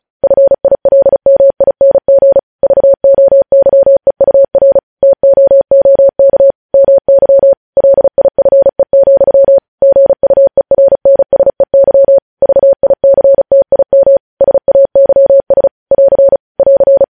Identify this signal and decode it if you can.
Morse code, heard as a keyed tone, about 36 wpm, 550 Hz. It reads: FILMING VOYEUR TOOK MY LIFE, GUERNSEY VICTIM SAYS <AR> <AR>